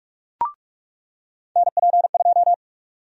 Morse code, heard as a keyed tone, about 46 wpm, 700 Hz. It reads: NP2
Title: USA Call Sign Prefixes - 45wpm